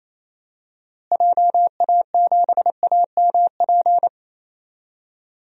Morse code, heard as a keyed tone, about 28 wpm, 700 Hz. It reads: JA7AMP
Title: DXCC-100 Call Signs - 6 Characters - Repeated 3x - 28wpm